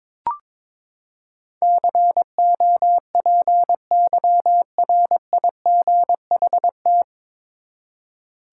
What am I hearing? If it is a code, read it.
Morse code, heard as a keyed tone, about 22 wpm, 700 Hz. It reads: COPYRIGHT
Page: Morse Code Ninja practice 22wpm